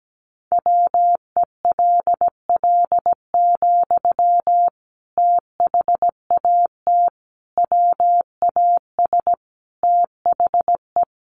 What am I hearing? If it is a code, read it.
Morse code, heard as a keyed tone, about 17 wpm, 700 Hz. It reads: WELL, THAT WAS THE
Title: Sentences from Top 1200 Words 17wpm